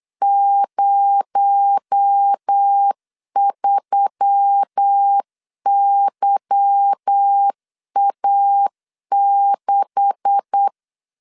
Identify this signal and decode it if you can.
Morse code, heard as a keyed tone, about 8 wpm, 800 Hz. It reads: 03YA6
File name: LEZIONI / 9 286 8wpm 3425